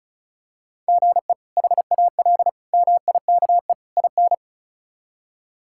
Morse code, heard as a keyed tone, about 34 wpm, 700 Hz. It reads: GE HAL MIKE IN
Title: K1USN SST QSO Practice - 35wpm